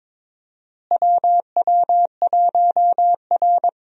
Morse code, heard as a keyed tone, about 22 wpm, 700 Hz. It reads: WW1R
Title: USA Call Signs - 4 Characters - Repeated 2x - 22wpm